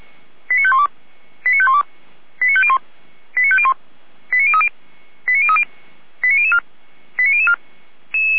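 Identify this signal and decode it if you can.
Five-tone selective call (ZVEI), heard as quick runs of five tones, tones of about 70 ms. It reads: 89512 89512 89501 89501 89030 89030 89004 89004
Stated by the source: ZVEI five-tone call 89010 received by SDR, Feuerwehr Orlamünde.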